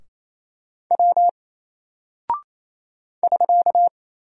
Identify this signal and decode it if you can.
Morse code, heard as a keyed tone, about 27 wpm, 700 Hz. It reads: W <SK>
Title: Single Character (Letter:Number:Punctuation:Prosign) 28wpm